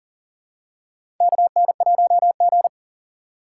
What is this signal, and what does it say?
Morse code, heard as a keyed tone, about 40 wpm, 700 Hz. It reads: KN1G
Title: USA Call Signs - 4 Characters - 8x Character Spacing - 40wpm